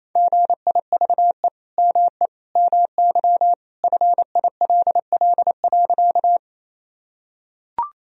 Morse code, heard as a keyed tone, about 28 wpm, 700 Hz. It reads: GIVE ME MY FILL.
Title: Sentences from Top 1000 Words - Review Part 3 - 28wpm